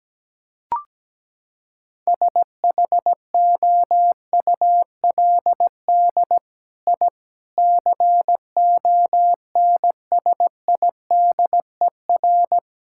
Morse code, heard as a keyed tone, about 17 wpm, 700 Hz. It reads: SHOULD I CONSIDER
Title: Sentences from Top 1000 Words - Review Part 5 - 17wpm